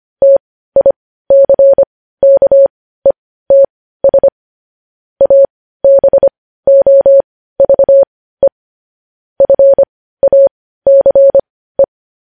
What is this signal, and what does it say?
Morse code, heard as a keyed tone, about 25 wpm, spaced as for 15 wpm, 550 Hz. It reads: TICKETS ABOVE FACE
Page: News Headlines in Morse Code at 15 WPM